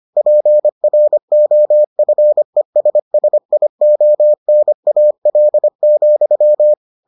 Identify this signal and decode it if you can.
Morse code, heard as a keyed tone, about 25 wpm, 600 Hz. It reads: PROFESSIONAL,